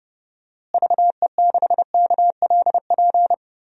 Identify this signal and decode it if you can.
Morse code, heard as a keyed tone, about 32 wpm, 700 Hz. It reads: VE6KLP
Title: DXCC-100 Call Signs - 6 Characters - Repeated 2x - 30wpm